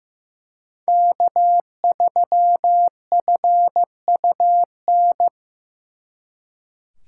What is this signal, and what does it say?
Morse code, heard as a keyed tone, about 18 wpm, 700 Hz. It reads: K3FUN
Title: Speed Racing - USA 1x3 Call Signs 15wpm